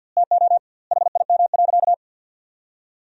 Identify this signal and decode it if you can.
Morse code, heard as a keyed tone, about 50 wpm, 700 Hz. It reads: TO HIM.
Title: 2x Word-Spacing - Sentences from Top 400 Words - 50wpm